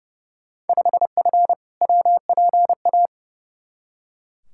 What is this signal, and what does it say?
Morse code, heard as a keyed tone, about 30 wpm, 700 Hz. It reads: 5F WPA
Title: Field Day Exchanges Encore - 30wpm